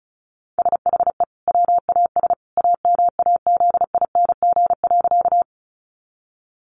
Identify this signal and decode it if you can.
Morse code, heard as a keyed tone, about 34 wpm, 700 Hz. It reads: SHE WAS AMAZING.
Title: Sentences from Top 1500 Words 35wpm